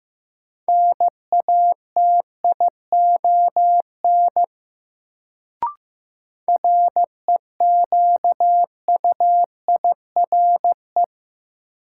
Morse code, heard as a keyed tone, about 15 wpm, 700 Hz. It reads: NATION REQUIRE
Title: Sentences from Top 900 Words - New Word Drill - 15wpm